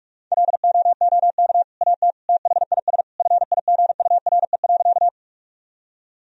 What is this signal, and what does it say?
Morse code, heard as a keyed tone, about 46 wpm, 700 Hz. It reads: LOOK AT THIS FIGURE.
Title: Sentences from Top 1000 Words - Review Part 2 - 45wpm